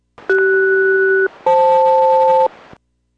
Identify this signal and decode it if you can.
Aircraft SELCAL, heard as two pairs of tones. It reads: CS-FL